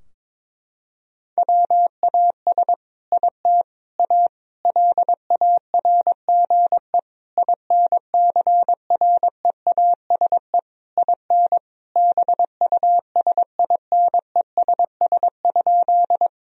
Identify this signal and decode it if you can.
Morse code, heard as a keyed tone, about 22 wpm, 700 Hz. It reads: WAS IT A LARGE INCREASE IN BUSINESS?
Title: Sentences from Top 1000 Words - Review Part 4 - 22wpm